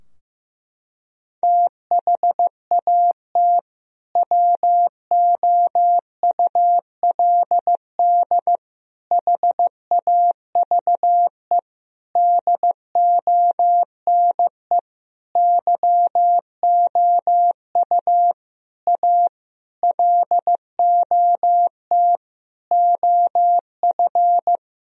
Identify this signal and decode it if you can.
Morse code, heard as a keyed tone, about 15 wpm, 700 Hz. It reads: THAT WOULD HAVE DONE YOU A LOT OF
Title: Sentences from Top 1000 Words - Review Part 3 - 15wpm